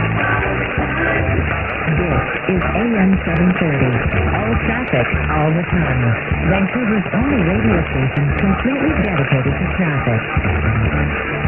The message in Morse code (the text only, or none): none